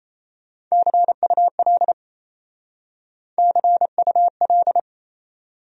Morse code, heard as a keyed tone, about 35 wpm, 700 Hz. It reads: CUL CUL
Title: Speed-Racing Abbreviations - 28wpm